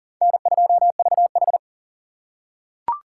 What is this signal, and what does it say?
Morse code, heard as a keyed tone, about 40 wpm, 700 Hz. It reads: N2VH
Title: USA Call Signs - 4 Characters - 2x Character Spacing - 40wpm